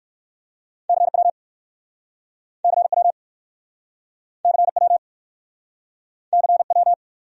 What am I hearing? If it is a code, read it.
Morse code, heard as a keyed tone, about 58 wpm, 700 Hz. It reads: CW CW CW CW